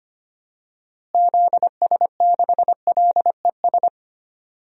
Morse code, heard as a keyed tone, about 25 wpm, 700 Hz. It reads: ZS6LES
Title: DXCC-100 Call Signs - 6 Characters - Repeated 3x - 25wpm